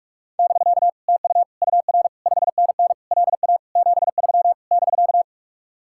Morse code, heard as a keyed tone, about 49 wpm, 700 Hz. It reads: <BK> TU UR 5NN LA 73 <BK>